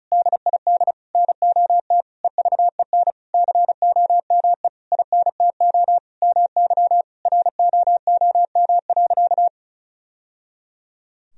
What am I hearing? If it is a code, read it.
Morse code, heard as a keyed tone, about 34 wpm, 700 Hz. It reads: DID NOT EVEN COME INTO MY ROOM.
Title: Sentences from Top 1000 Words - Review Part 1 - 35wpm